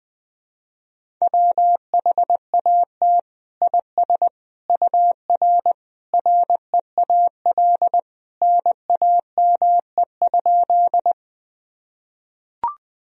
Morse code, heard as a keyed tone, about 20 wpm, 700 Hz. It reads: WHAT IS UR REAL NAME?
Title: Sentences from Top 300 Words and Abbreviations 20wpm